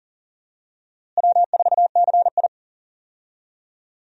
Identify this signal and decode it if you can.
Morse code, heard as a keyed tone, about 40 wpm, 700 Hz. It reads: W4CI
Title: USA Call Signs - 4 Characters - Repeated 2x - 40wpm